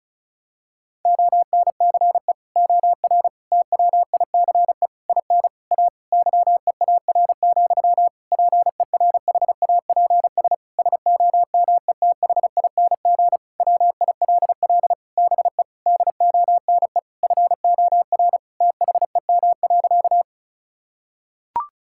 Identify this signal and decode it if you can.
Morse code, heard as a keyed tone, about 36 wpm, 700 Hz. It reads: ONCE OR TWICE IN A YEAR, PERHAPS SOMETHING WILL BE DONE FOR THEM.
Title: Sentences from Top 1600 Words 35wpm